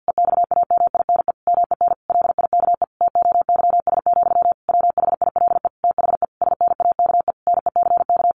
Morse code, heard as a keyed tone, about 48 wpm, 700 Hz. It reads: EXAMINE MEN LIKE TOYS, WHILE THE SNAKE NECK
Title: A Sound Of Thunder 50wpm